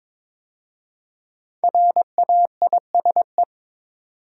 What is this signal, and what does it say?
Morse code, heard as a keyed tone, about 22 wpm, 700 Hz. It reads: RAISE
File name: 5 Letter Words - First 3 Letters Given - 22wpm